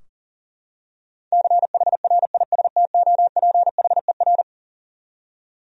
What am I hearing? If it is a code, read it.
Morse code, heard as a keyed tone, about 37 wpm, 700 Hz. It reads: CHRISTOPHER
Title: USA Common Names - 40wpm